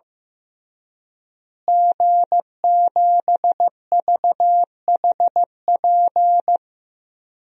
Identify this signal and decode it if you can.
Morse code, heard as a keyed tone, about 15 wpm, 700 Hz. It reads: G7VHP
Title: DXCC-100 Call Signs - 5 Characters - Repeated 2x - 15wpm